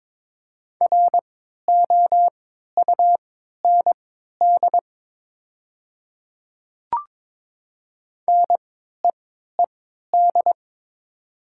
Morse code, heard as a keyed tone, about 22 wpm, spaced as for 12 wpm, 700 Hz. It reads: ROUND NEED
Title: 3x Character Spacing - Top 500 Words - 22wpm